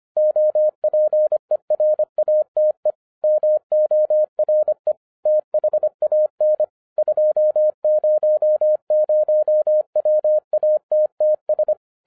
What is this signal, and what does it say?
Morse code, heard as a keyed tone, about 25 wpm, 610 Hz. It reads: OPERATE MORE THAN 200 WATTS
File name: 2012-01-mp3 / n08-ITS MY FAULT, BUT-25wpm